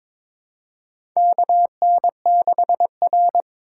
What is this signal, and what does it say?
Morse code, heard as a keyed tone, about 21 wpm, 700 Hz. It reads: KN6R
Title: USA Call Signs - 4 Characters - 3x Character Spacing - 22wpm